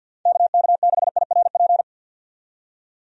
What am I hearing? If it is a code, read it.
Morse code, heard as a keyed tone, about 51 wpm, 700 Hz. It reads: KK6IRP
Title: USA Call Signs - 6 Characters - 50wpm